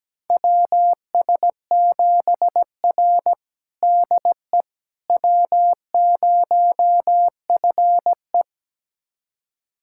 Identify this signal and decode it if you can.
Morse code, heard as a keyed tone, about 17 wpm, 700 Hz. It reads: WS7R DE W0FE
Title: USA Callsign DE Callsign - 4 Characters - 17wpm